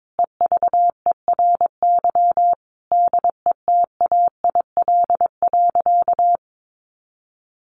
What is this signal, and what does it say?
Morse code, heard as a keyed tone, about 22 wpm, 700 Hz. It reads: EVERY DETAIL.
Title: Sentences from Top 1500 Words 22wpm